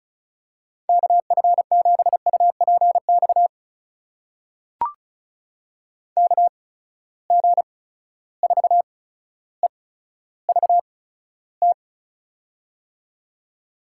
Morse code, heard as a keyed tone, about 36 wpm, spaced as for 15 wpm, 700 Hz. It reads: KF7UPX KG4EVT
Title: USA Call Signs - 6 Characters - 8x Character Spacing - 35wpm